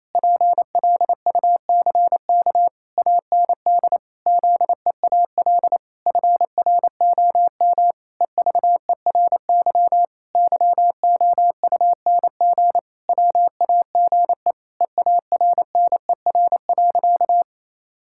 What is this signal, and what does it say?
Morse code, heard as a keyed tone, about 28 wpm, 700 Hz. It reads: PLUCK AND ZEAL FROM EVERY YOUNG WAGE EARNER.
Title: Pangram Challenge v2 - 28wpm